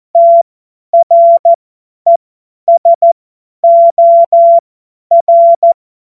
Morse code, heard as a keyed tone, about 14 wpm, spaced as for 10 wpm, 680 Hz. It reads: TRESOR
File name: Morsekurs / Witze / Witz 29 April 2025 50 F70